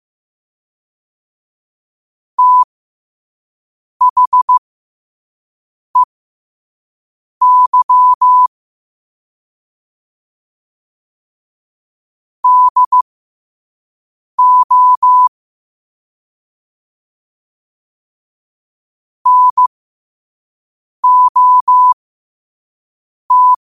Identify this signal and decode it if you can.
Morse code, heard as a keyed tone, about 15 wpm, spaced as for 5 wpm, 1000 Hz. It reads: THEY DO NOT
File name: qotdfiles / 5wpm